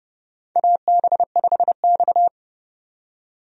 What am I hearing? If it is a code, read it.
Morse code, heard as a keyed tone, about 30 wpm, 700 Hz. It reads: AB5X
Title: USA Call Signs - 4 Characters - 30wpm